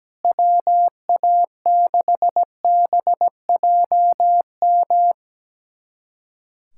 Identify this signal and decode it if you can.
Morse code, heard as a keyed tone, about 17 wpm, 700 Hz. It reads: WA6BJM
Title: USA Callsign DE Callsign 17wpm